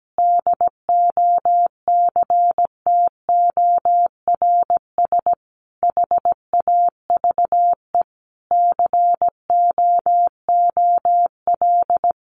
Morse code, heard as a keyed tone, about 17 wpm, 700 Hz. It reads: DOCTORS HAVE COOL
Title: Easy Phrases 17wpm